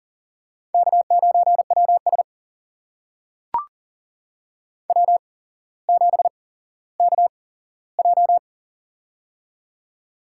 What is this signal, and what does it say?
Morse code, heard as a keyed tone, about 40 wpm, spaced as for 17 wpm, 700 Hz. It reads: K9WS W7KJ
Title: USA Call Signs - 4 Characters - 8x Character Spacing - 40wpm